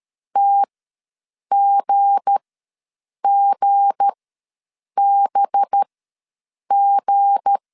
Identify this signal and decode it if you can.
Morse code, heard as a keyed tone, about 13 wpm, spaced as for 7 wpm, 800 Hz. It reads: TGGBG
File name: LEZIONI / 7 GF9Z / 7 G ws 12wpm 3140